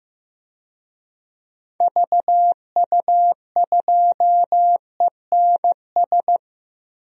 Morse code, heard as a keyed tone, about 15 wpm, 700 Hz. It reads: VU2ENS
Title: DXCC-100 Call Signs - 6 Characters - Repeated 3x - 15wpm